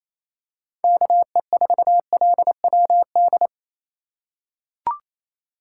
Morse code, heard as a keyed tone, about 28 wpm, 700 Hz. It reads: KE4LWD
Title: USA Call Signs - 6 Characters - 2x Character Spacing - 28wpm